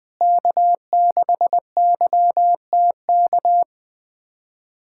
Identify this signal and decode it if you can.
Morse code, heard as a keyed tone, about 21 wpm, 700 Hz. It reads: K6YTK
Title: USA Callsign DE Callsign - 5 Characters - 20wpm